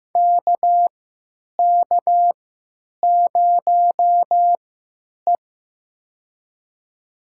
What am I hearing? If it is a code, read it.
Morse code, heard as a keyed tone, about 15 wpm, spaced as for 8 wpm, 700 Hz. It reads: KK0E
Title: USA Call Signs - 3x Character Spacing - 15wpm